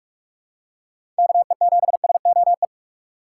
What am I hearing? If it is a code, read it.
Morse code, heard as a keyed tone, about 46 wpm, 700 Hz. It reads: KE7SOE
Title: USA Call Signs - 6 Characters - 3x Character Spacing - 45wpm